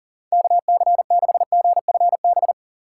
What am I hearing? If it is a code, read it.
Morse code, heard as a keyed tone, about 40 wpm, 700 Hz. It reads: KC6GFB
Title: USA Call Signs - 8x Character Spacing - 40wpm